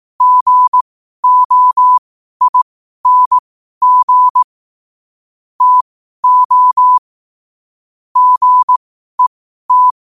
Morse code, heard as a keyed tone, about 18 wpm, spaced as for 12 wpm, 1000 Hz. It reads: GOING TO GET
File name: qotdfiles / 12wpm